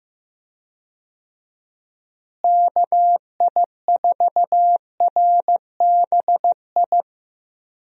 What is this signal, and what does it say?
Morse code, heard as a keyed tone, about 15 wpm, 700 Hz. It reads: KI4RBI
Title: USA Call Signs - 6 Characters - Repeated 2x - 15wpm